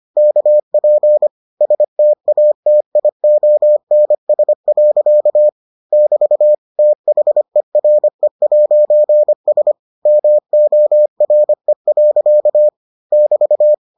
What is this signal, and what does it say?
Morse code, heard as a keyed tone, about 25 wpm, 600 Hz. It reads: KP STATIONS. <BT> THERE'S MORE. <BT>